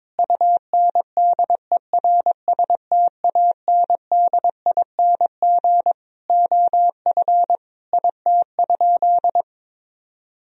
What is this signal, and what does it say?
Morse code, heard as a keyed tone, about 21 wpm, 700 Hz. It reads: UNDERSTANDING OF IT?
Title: Sentences from Top 1000 Words - Review Part 3 - 22wpm